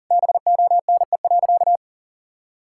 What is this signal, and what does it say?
Morse code, heard as a keyed tone, about 40 wpm, 700 Hz. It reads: BONE.
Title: Sentences from Top 1600 Words 40wpm